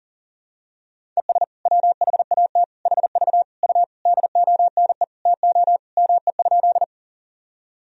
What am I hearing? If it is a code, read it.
Morse code, heard as a keyed tone, about 37 wpm, 700 Hz. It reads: ES WHAT HV U DONE TO ME?
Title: Sentences from Top 500 Words and Abbreviations 40wpm